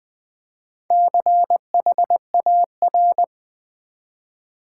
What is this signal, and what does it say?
Morse code, heard as a keyed tone, about 20 wpm, 700 Hz. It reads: CHAR
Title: Build A Word - 1 Letter At A Time - 5-Letter Words - 20wpm